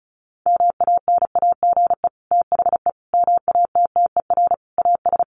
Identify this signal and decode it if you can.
Morse code, heard as a keyed tone, about 34 wpm, 700 Hz. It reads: MANAGE THE MATTER AS
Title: Sentences from Top 1400 Words 35wpm